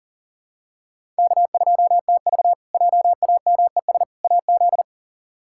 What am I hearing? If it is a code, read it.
Morse code, heard as a keyed tone, about 40 wpm, 700 Hz. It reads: K2TV JAMES AZ
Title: K1USN SST QSO Practice - 40wpm